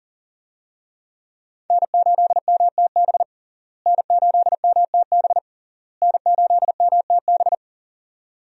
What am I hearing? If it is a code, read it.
Morse code, heard as a keyed tone, about 40 wpm, 700 Hz. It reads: N8MTB N8MTB N8MTB
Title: USA Call Signs - 5 Characters - Repeated 3x - 40wpm